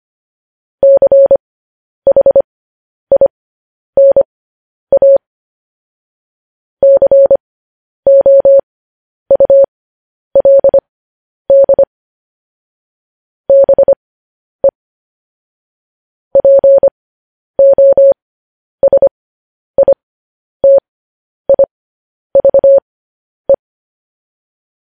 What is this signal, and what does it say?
Morse code, heard as a keyed tone, about 25 wpm, spaced as for 10 wpm, 550 Hz. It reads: CHINA COULD BE POSITIVE